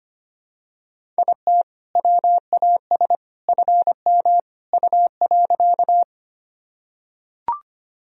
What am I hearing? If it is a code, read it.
Morse code, heard as a keyed tone, about 25 wpm, 700 Hz. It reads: IT WAS FM U.
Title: Sentences from Top 100 Words and Abbreviations 25wpm